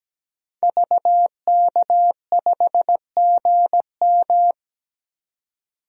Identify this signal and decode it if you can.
Morse code, heard as a keyed tone, about 17 wpm, 700 Hz. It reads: VK5GM
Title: DXCC-100 Call Signs - 5 Characters - 17wpm